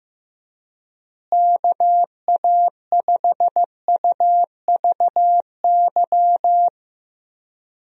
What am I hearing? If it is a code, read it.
Morse code, heard as a keyed tone, about 15 wpm, 700 Hz. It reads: KA5UVY
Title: USA Call Signs - 6 Characters - Repeated 2x - 15wpm